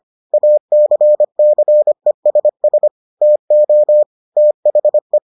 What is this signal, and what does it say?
Morse code, heard as a keyed tone, about 25 wpm, 600 Hz. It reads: ACCESS TO THE